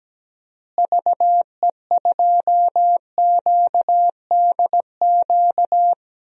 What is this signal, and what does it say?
Morse code, heard as a keyed tone, about 17 wpm, 700 Hz. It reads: VE2QDQ
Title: DXCC-100 Call Signs - 6 Characters - Repeated 2x - 17wpm